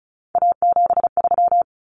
Morse code, heard as a keyed tone, about 34 wpm, 700 Hz. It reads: A73
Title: Common Cut Numbers - 3-Digit 35wpm